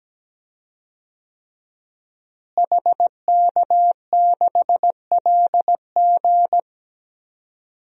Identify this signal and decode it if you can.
Morse code, heard as a keyed tone, about 17 wpm, 700 Hz. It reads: HK6LG
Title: DXCC-100 Call Signs - 5 Characters - Repeated 3x - 17wpm